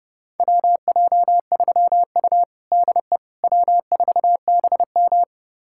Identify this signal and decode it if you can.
Morse code, heard as a keyed tone, about 31 wpm, 700 Hz. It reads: WJ3U DE W4BM
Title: USA Callsign DE Callsign - 4 Characters - 30wpm